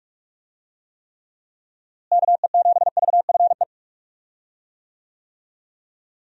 Morse code, heard as a keyed tone, about 46 wpm, 700 Hz. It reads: KE7VFE
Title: USA Call Signs - 6 Characters - Repeated 3x - 45wpm